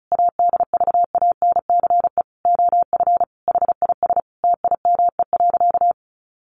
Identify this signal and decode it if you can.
Morse code, heard as a keyed tone, about 34 wpm, 700 Hz. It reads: ADVANCE OF HIS TIME.
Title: Sentences from Top 1400 Words 35wpm